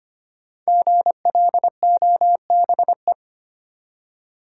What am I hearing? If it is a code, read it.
Morse code, heard as a keyed tone, about 25 wpm, 690 Hz. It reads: GLOBE
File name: practice / 25wpm / Easy Words with Hints - 25wpm